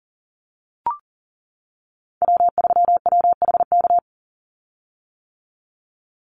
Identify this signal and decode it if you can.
Morse code, heard as a keyed tone, about 40 wpm, 700 Hz. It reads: W3WHK
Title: Call Signs - 40wpm